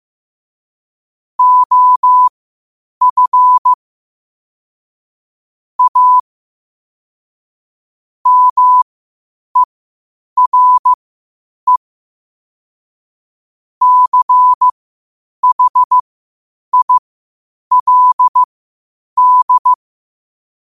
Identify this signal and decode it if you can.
Morse code, heard as a keyed tone, about 15 wpm, spaced as for 8 wpm, 1000 Hz. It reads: OF A MERE CHILD
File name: qotdfiles / 8wpm